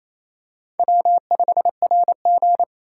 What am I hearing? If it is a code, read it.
Morse code, heard as a keyed tone, about 28 wpm, 700 Hz. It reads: W5RG
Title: USA Call Signs - 4 Characters - 4x Character Spacing - 28wpm